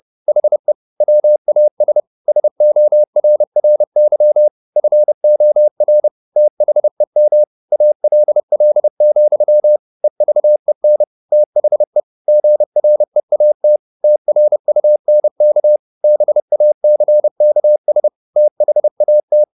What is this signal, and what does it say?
Morse code, heard as a keyed tone, about 30 wpm, 600 Hz. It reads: HE WAS SORRY FOR THEM ALL, EVEN THE GREAT TRUNK BACKS THAT
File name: the old man and the sea / the old man and the sea 1 30wpm